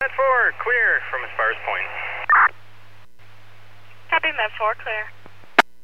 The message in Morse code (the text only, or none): none